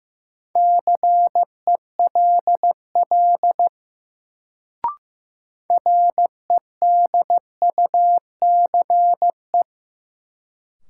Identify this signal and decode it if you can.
Morse code, heard as a keyed tone, about 15 wpm, 700 Hz. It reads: CELL REDUCE
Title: Sentences from Top 1100 Words - New Word Drill - 15wpm